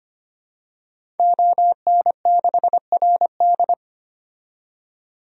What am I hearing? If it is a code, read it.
Morse code, heard as a keyed tone, about 25 wpm, 700 Hz. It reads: ON6RD
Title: DXCC-100 Call Signs - 5 Characters - 25wpm